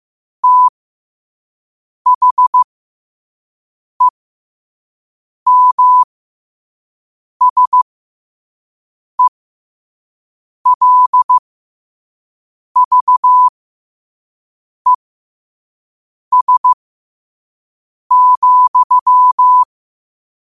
Morse code, heard as a keyed tone, about 15 wpm, spaced as for 5 wpm, 1000 Hz. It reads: THEMSELVES,
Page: New quotes every day in morse code at 5 Words per minute.